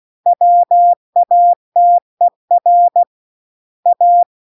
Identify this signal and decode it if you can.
Morse code, heard as a keyed tone, about 16 wpm, 700 Hz. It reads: WATER A